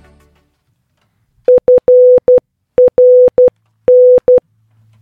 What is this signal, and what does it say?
Morse code, heard as a keyed tone, about 12 wpm, 500 Hz. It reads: FRN